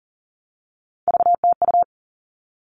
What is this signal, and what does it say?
Morse code, heard as a keyed tone, about 40 wpm, 700 Hz. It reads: VTU
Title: All Cut Numbers - 3-Digit 40wpm